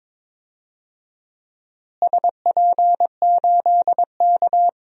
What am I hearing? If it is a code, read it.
Morse code, heard as a keyed tone, about 21 wpm, 700 Hz. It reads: SP8K
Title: DXCC-100 Call Signs - 4 Characters - Repeated 3x - 22wpm